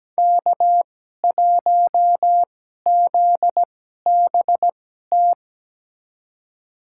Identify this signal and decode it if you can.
Morse code, heard as a keyed tone, about 17 wpm, spaced as for 12 wpm, 700 Hz. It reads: K1ZBT
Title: USA Call Signs - 2x Character Spacing - 17wpm